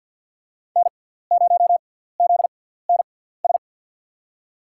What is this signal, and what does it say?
Morse code, heard as a keyed tone, about 48 wpm, spaced as for 17 wpm, 700 Hz. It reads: N0ZNS
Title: USA Call Signs - 6x Character Spacing - 50wpm